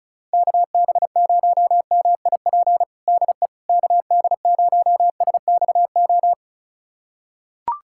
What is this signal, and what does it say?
Morse code, heard as a keyed tone, about 36 wpm, 700 Hz. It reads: KB0MIP DE KD0SXO
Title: USA Callsign DE Callsign 35wpm